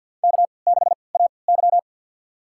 Morse code, heard as a keyed tone, about 52 wpm, spaced as for 28 wpm, 700 Hz. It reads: K6AY
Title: USA Call Signs - 4 Characters - 3x Character Spacing - 50wpm